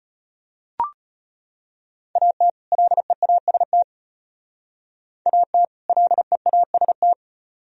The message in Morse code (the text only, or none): AT LEAST AT LEAST